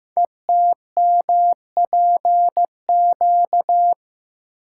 Morse code, heard as a keyed tone, about 15 wpm, 700 Hz. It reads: ETMPQ
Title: Code Groups - 5 Letters - 15wpm